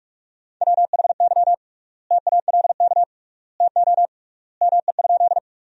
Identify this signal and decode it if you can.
Morse code, heard as a keyed tone, about 46 wpm, 700 Hz. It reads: WHY TALK TO ME?
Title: 3x Word-Spacing - Sentences from Top 400 Words - 45wpm